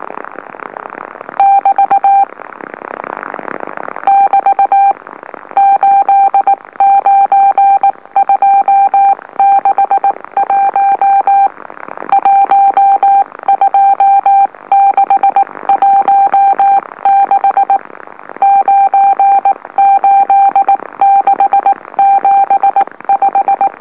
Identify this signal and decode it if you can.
Morse code, heard as a keyed tone, about 19 wpm, spaced as for 16 wpm, 790 Hz. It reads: <BT> <BT> 89261 12616 98675
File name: M04- -2345utc-LOLO